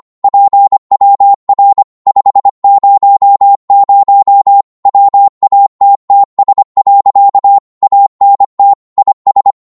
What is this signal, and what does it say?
Morse code, heard as a keyed tone, about 25 wpm, 800 Hz. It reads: PWR 500 WATTS. ANT IS
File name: Morse / qso-25wpm